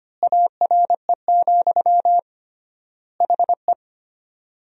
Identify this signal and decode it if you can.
Morse code, heard as a keyed tone, about 25 wpm, 700 Hz. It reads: ARE, HE
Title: 3x Word-Spacing - Sentences from Top 100 Words - 25wpm